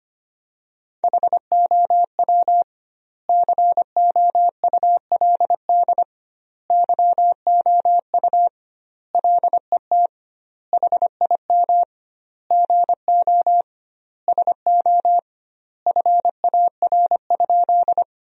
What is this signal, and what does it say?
Morse code, heard as a keyed tone, about 25 wpm, 700 Hz. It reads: HOW COULD YOU LET HIM GO SO FAR?
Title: 2x Word-Spacing - Sentences from Top 300 Words - 25wpm